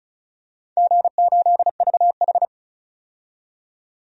Morse code, heard as a keyed tone, about 36 wpm, 700 Hz. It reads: G8VH
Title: DXCC-100 Call Signs - 4 Characters - 35wpm